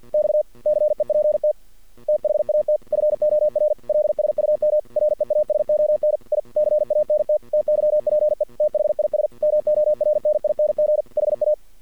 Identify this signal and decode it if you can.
Morse code, heard as a keyed tone, about 48 wpm, 600 Hz. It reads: X CEPT THAT FOR LSUM REASON I CANT TYPE THIS MORNING HA
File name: archived / CRSnet